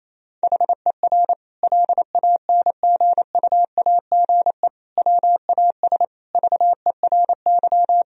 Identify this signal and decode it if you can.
Morse code, heard as a keyed tone, about 28 wpm, 700 Hz. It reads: HER LANGUAGE WAS VERY
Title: Sentences from Top 1000 Words - Review Part 3 - 28wpm